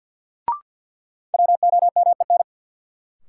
Morse code, heard as a keyed tone, about 52 wpm, 700 Hz. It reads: WOMEN
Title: Sentences from Top 1300 Words - New Word Drill - 50wpm